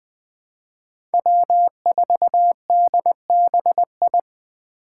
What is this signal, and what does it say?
Morse code, heard as a keyed tone, about 20 wpm, 700 Hz. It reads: W4DBI
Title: Five Challenging Letter-Number Combinations 20wpm